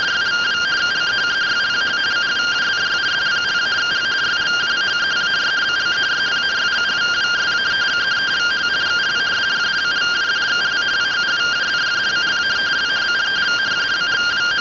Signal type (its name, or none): two-tone teleprinter signal